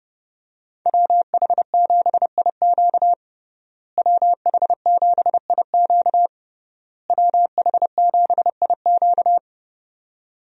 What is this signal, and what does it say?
Morse code, heard as a keyed tone, about 31 wpm, 700 Hz. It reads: WH7IQ WH7IQ WH7IQ
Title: USA Call Signs - 5 Characters - Repeated 3x - 30wpm